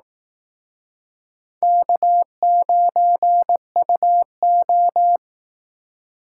Morse code, heard as a keyed tone, about 18 wpm, 700 Hz. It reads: K9UO
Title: Speed Racing - USA 1x2 Call Signs 15wpm